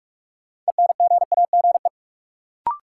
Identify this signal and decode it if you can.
Morse code, heard as a keyed tone, about 42 wpm, 700 Hz. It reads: ENGAGE
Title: Sentences from Top 1500 Words - New Word Drill - 45wpm